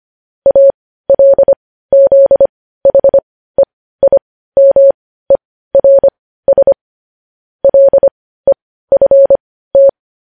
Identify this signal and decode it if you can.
Morse code, heard as a keyed tone, about 25 wpm, spaced as for 15 wpm, 550 Hz. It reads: ALZHEIMERS LEFT
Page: News headlines in Morse code at 15 wpm